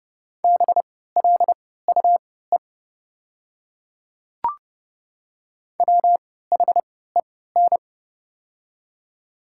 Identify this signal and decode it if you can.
Morse code, heard as a keyed tone, about 30 wpm, spaced as for 17 wpm, 690 Hz. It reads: BLUE WHEN
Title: 3x Character Spacing - Top 500 Words - 30wpm